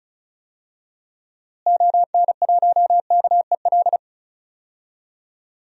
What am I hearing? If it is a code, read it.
Morse code, heard as a keyed tone, about 34 wpm, 700 Hz. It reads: ON1KEL
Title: DXCC-100 Call Signs - 6 Characters - Repeated 3x - 35wpm